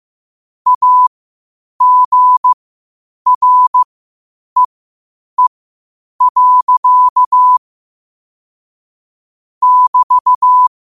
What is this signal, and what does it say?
Morse code, heard as a keyed tone, about 15 wpm, spaced as for 8 wpm, 1000 Hz. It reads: AGREE. <BT>
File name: qotdfiles / 8wpm